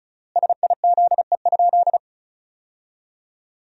Morse code, heard as a keyed tone, about 34 wpm, 700 Hz. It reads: SIZE?
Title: 3x Word-Spacing - Sentences from Top 500 Words - 35wpm